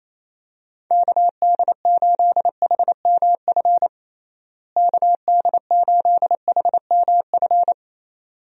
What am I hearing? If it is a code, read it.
Morse code, heard as a keyed tone, about 28 wpm, 700 Hz. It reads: KD8HMF KD8HMF
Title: USA Call Signs - 6 Characters - Repeated 2x - 28wpm